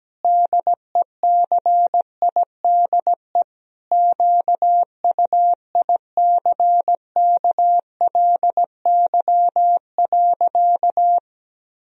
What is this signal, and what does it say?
Morse code, heard as a keyed tone, about 17 wpm, 700 Hz. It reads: DECIDE QUICKLY.
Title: Sentences from Top 1000 Words - Review Part 3 - 17wpm